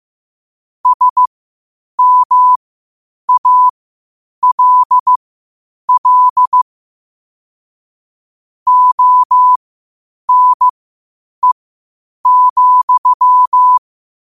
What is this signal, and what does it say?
Morse code, heard as a keyed tone, about 15 wpm, spaced as for 8 wpm, 1000 Hz. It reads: SMALL ONE,
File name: qotdfiles / 8wpm